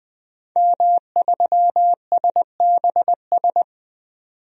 Morse code, heard as a keyed tone, about 20 wpm, 700 Hz. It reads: M3SBS
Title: DXCC-100 Call Signs - 5 Characters - 20wpm